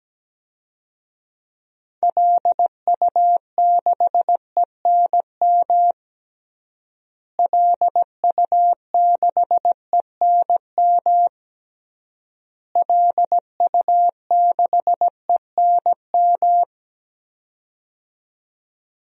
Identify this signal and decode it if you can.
Morse code, heard as a keyed tone, about 17 wpm, 700 Hz. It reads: LU6ENM LU6ENM LU6ENM
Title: DXCC-100 Call Signs - 6 Characters - Repeated 3x - 17wpm